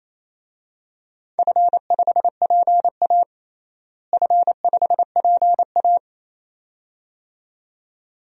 Morse code, heard as a keyed tone, about 28 wpm, 700 Hz. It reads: F5PA F5PA
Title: DXCC-100 Call Signs - 4 Characters - Repeated 2x - 28wpm